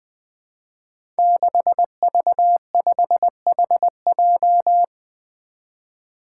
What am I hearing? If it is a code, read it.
Morse code, heard as a keyed tone, about 20 wpm, 700 Hz. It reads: 6V5HJ
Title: Five Challenging Letter-Number Combinations 20wpm